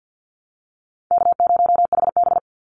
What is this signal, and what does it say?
Morse code, heard as a keyed tone, about 50 wpm, 700 Hz. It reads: K0HB
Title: Easy Call Signs Repeated 3x - 50wpm